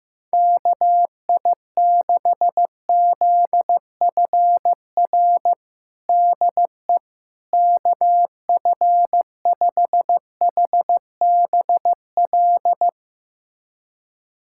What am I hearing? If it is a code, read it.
Morse code, heard as a keyed tone, about 16 wpm, 700 Hz. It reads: KI6ZFR DE KF5HBL
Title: USA Callsign DE Callsign - 6 Characters - 15wpm